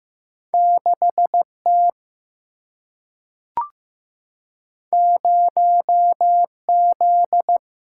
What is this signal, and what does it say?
Morse code, heard as a keyed tone, about 15 wpm, 700 Hz. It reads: 6 T 0 Z
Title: Single-Letter Single-Number 15wpm